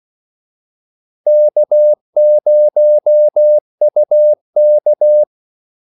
Morse code, heard as a keyed tone, about 16 wpm, 600 Hz. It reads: K0UK
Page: Kaldesignaler 16 wpm